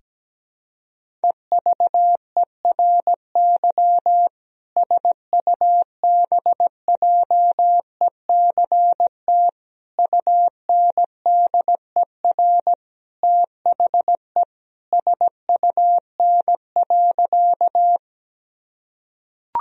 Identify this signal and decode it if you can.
Morse code, heard as a keyed tone, about 17 wpm, 700 Hz. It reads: EVERY SUBJECT UNDER THE SUN.
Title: Sentences from Top 1000 Words - Review Part 1 - 17wpm